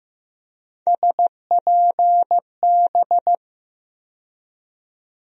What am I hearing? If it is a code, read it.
Morse code, heard as a keyed tone, about 15 wpm, 700 Hz. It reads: SPB
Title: USA Call Sign Postfixes - 3 Letters 15wpm